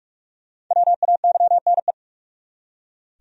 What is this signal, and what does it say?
Morse code, heard as a keyed tone, about 46 wpm, 700 Hz. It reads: WAYNE
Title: USA Common Names - Max 5 Letters - 45wpm